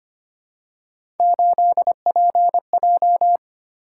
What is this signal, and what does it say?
Morse code, heard as a keyed tone, about 25 wpm, 700 Hz. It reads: 8PJ
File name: Three Letter-Number Combinations - 25wpm